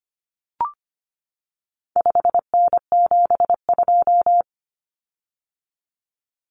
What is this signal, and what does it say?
Morse code, heard as a keyed tone, about 25 wpm, 700 Hz. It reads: 5N72
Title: Common Cut Numbers - 4-Digit 25wpm